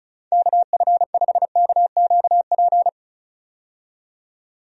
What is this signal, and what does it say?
Morse code, heard as a keyed tone, about 36 wpm, 700 Hz. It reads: KF5KQP
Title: USA Call Signs - 6 Characters - 35wpm